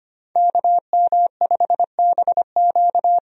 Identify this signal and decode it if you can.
Morse code, heard as a keyed tone, about 25 wpm, 700 Hz. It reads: KM5BQ
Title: USA Call Signs - 5 Characters - 25wpm